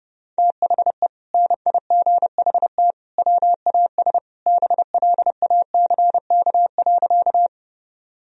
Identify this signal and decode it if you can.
Morse code, heard as a keyed tone, about 30 wpm, 700 Hz. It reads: THE NIGHT WAS BLACK.
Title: Sentences from Top 1000 Words - Review Part 2 - 30wpm